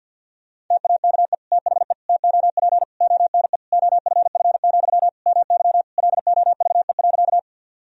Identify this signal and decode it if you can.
Morse code, heard as a keyed tone, about 51 wpm, 700 Hz. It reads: TAKE THE TOP ONE OFF, MY LOVE.
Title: Sentences from Top 1000 Words - Review Part 2 - 50wpm